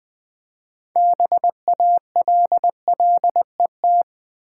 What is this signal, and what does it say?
Morse code, heard as a keyed tone, about 20 wpm, 700 Hz. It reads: BALLET
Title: 6 Letter Words - First 3 Letters Given - 20wpm